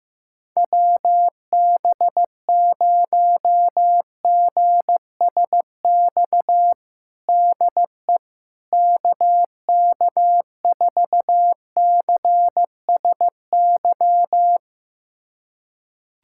Morse code, heard as a keyed tone, about 16 wpm, 700 Hz. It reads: WB0GSX DE KK4CSY
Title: USA Callsign DE Callsign - 6 Characters - 15wpm